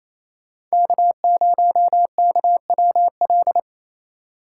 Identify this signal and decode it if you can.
Morse code, heard as a keyed tone, about 36 wpm, 700 Hz. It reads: K0KWL